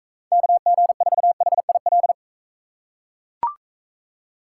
Morse code, heard as a keyed tone, about 43 wpm, 700 Hz. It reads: KG4HIL